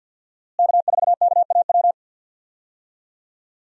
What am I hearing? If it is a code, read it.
Morse code, heard as a keyed tone, about 51 wpm, 700 Hz. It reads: K4KAW
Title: USA Call Signs - 5 Characters - Repeated 2x - 50wpm